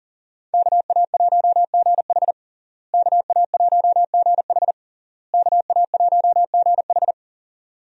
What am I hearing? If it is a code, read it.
Morse code, heard as a keyed tone, about 40 wpm, 700 Hz. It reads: KA1GH KA1GH KA1GH
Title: USA Call Signs - 5 Characters - Repeated 3x - 40wpm